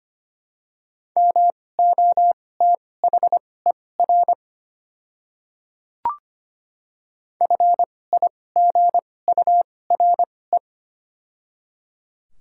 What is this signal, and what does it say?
Morse code, heard as a keyed tone, about 25 wpm, spaced as for 18 wpm, 700 Hz. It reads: MOTHER FIGURE
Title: 2x Character Spacing - Top 500 Words - 25wpm